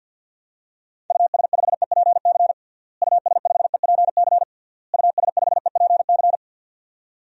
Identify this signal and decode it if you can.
Morse code, heard as a keyed tone, about 53 wpm, 700 Hz. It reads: US5EPC US5EPC US5EPC
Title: DXCC-100 Call Signs - 6 Characters - Repeated 3x - 50wpm